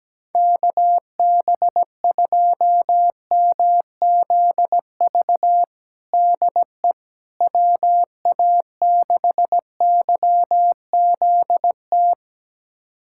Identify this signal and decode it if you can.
Morse code, heard as a keyed tone, about 17 wpm, 700 Hz. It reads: KB2MZV DE WA6YZT
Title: USA Callsign DE Callsign - 6 Characters - 17wpm